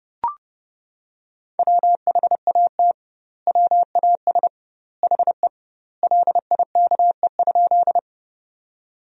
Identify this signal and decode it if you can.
Morse code, heard as a keyed tone, about 30 wpm, 700 Hz. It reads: WHAT WAS HE LIKE?
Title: 2x Word-Spacing - Sentences from Top 100 Words - 30wpm